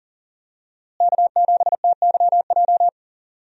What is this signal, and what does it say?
Morse code, heard as a keyed tone, about 38 wpm, 700 Hz. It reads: K7TYJ
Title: USA Call Signs - 5 Characters - Repeated 2x - 40wpm